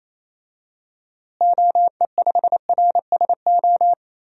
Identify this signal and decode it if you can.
Morse code, heard as a keyed tone, about 27 wpm, 700 Hz. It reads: OE5RSO
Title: DXCC-100 Call Signs - 6 Characters - Repeated 2x - 28wpm